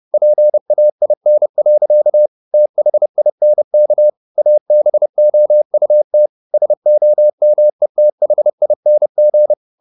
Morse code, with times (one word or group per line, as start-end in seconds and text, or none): 0.1-2.3 PAIN.
2.5-4.1 THINK
4.4-6.3 ABOUT
6.5-9.5 SOMETHING